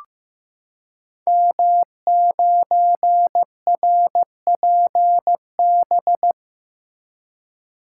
Morse code, heard as a keyed tone, about 15 wpm, 700 Hz. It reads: M9RPB
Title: Code Groups - 5 Letter:Numbers - 15wpm